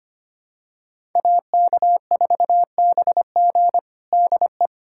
Morse code, heard as a keyed tone, about 25 wpm, 700 Hz. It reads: AK4BG DE
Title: USA Callsign DE Callsign - 5 Characters - 25wpm